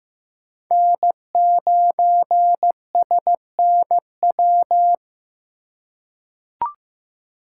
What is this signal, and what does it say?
Morse code, heard as a keyed tone, about 15 wpm, 700 Hz. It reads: N9SNW
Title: USA Call Signs - 5 Characters - 8x Character Spacing - 15wpm